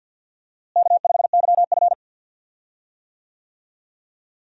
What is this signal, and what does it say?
Morse code, heard as a keyed tone, about 50 wpm, 700 Hz. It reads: K5YF
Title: USA Call Signs - 4 Characters - Repeated 2x - 50wpm